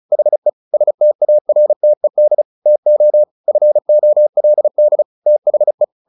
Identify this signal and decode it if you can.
Morse code, heard as a keyed tone, about 34 wpm, 600 Hz. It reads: HE STARTED TO FOLD THE
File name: the old man and the sea / the old man and the sea 1 35wpm